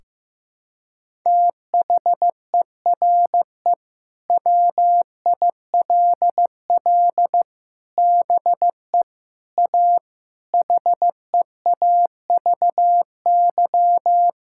Morse code, heard as a keyed tone, about 14 wpm, 700 Hz. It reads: THERE WILL BE A HEAVY
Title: Sentences from Top 1600 Words 15wpm